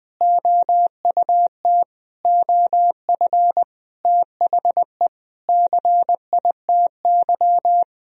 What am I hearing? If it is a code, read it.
Morse code, heard as a keyed tone, about 20 wpm, 700 Hz. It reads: OUT OF THE CITY
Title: Sentences from Top 1000 Words - Review Part 1 - 20wpm